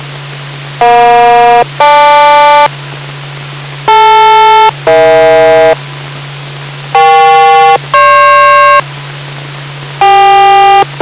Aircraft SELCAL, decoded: EJ-GL DL-EH EK-GP